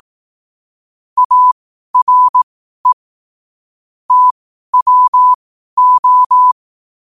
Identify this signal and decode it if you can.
Morse code, heard as a keyed tone, about 18 wpm, spaced as for 12 wpm, 1000 Hz. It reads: ARE TWO
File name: qotdfiles / 12wpm